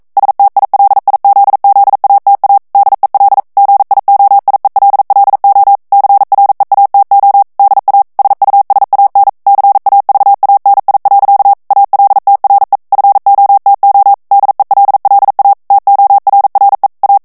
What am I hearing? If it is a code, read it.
Morse code, heard as a keyed tone, about 42 wpm, 800 Hz. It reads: STILIZZATA DEL GIOIELLO CREATO DA SUSAN CAVANI. ALTRE FOTO DELLA TORRE A
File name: morse mp3 / ITALIEN texte clair 1 vitesse 40 mots minute